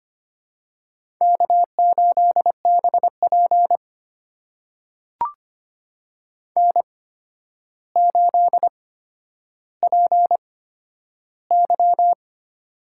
Morse code, heard as a keyed tone, about 25 wpm, spaced as for 11 wpm, 700 Hz. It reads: K8BP N8PY
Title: USA Call Signs - 4 Characters - 8x Character Spacing - 25wpm